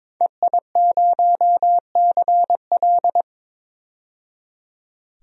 Morse code, heard as a keyed tone, about 22 wpm, 700 Hz. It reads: EI0CL
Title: DXCC-100 Call Signs - 5 Characters - 22wpm